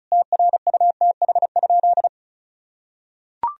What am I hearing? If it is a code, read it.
Morse code, heard as a keyed tone, about 36 wpm, 700 Hz. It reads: TRUTH?
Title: Sentences from Top 1000 Words - Review Part 1 - 35wpm